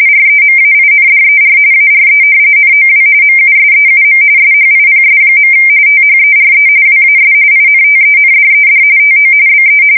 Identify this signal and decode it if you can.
RTTY, about 45.45 baud. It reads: REMAINDER ARE NOW MUSEUM SHIPS. THERE WAS INTENDED TO BE
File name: fsk170 / battleships